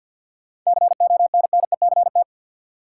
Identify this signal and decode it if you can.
Morse code, heard as a keyed tone, about 53 wpm, 700 Hz. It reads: CONNECT
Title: Sentences from Top 1500 Words - New Word Drill Alt - 50wpm